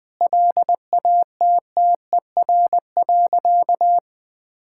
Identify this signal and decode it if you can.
Morse code, heard as a keyed tone, about 20 wpm, 700 Hz. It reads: LATTER.
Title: Sentences from Top 1600 Words 20wpm